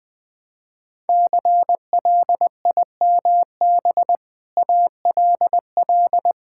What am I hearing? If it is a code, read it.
Morse code, heard as a keyed tone, about 19 wpm, 700 Hz. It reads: CLIMB ALL
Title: Sentences with 6-Word Phrases - Spoken Beginning - 20wpm